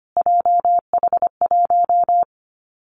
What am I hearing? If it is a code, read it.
Morse code, heard as a keyed tone, about 25 wpm, 700 Hz. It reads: JH1
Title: Callsign Prefixes for Top 25 DXCC - Repeated 3x - 25wpm